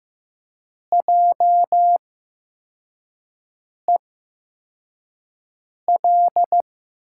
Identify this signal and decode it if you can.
Morse code, heard as a keyed tone, about 15 wpm, spaced as for 4 wpm, 700 Hz. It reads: JEL
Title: USA Call Signs - 8x Character Spacing - 15wpm